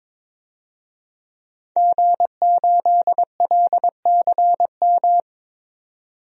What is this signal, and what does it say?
Morse code, heard as a keyed tone, about 22 wpm, 700 Hz. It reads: G8LCM
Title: DXCC-100 Call Signs - 5 Characters - 22wpm